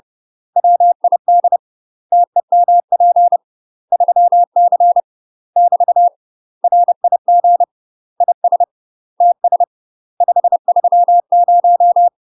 Morse code, heard as a keyed tone, about 30 wpm, 700 Hz. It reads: WID TEMP 3C <BT> RIG IS TS 530
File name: UK MASTER QSOs TRAINING / 30WPM / UK MASTER QSOs 0002